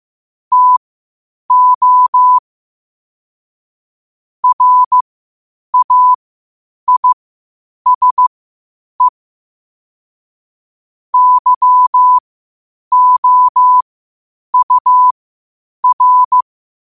Morse code, heard as a keyed tone, about 15 wpm, spaced as for 8 wpm, 1000 Hz. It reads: TO RAISE YOUR